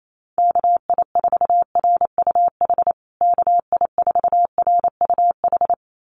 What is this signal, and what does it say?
Morse code, heard as a keyed tone, about 28 wpm, 700 Hz. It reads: KI4RUH KI4RUH
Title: USA Call Signs Repeated 2x 28wpm